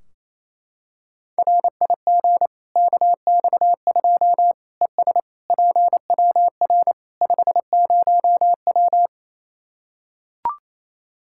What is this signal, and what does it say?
Morse code, heard as a keyed tone, about 28 wpm, 700 Hz. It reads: RIG KX2 ES PWR 50W
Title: Practice QSO - Exchange 2 - Info - 28wpm